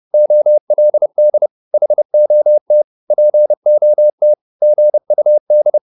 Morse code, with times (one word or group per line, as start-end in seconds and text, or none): 0.1-1.5 OLD
1.7-2.8 HOT
3.1-4.3 POT
4.6-5.8 GUD